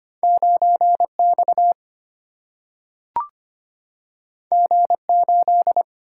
Morse code, heard as a keyed tone, about 25 wpm, 700 Hz. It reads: 9 X G 8
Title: Single-Letter Single-Number 25wpm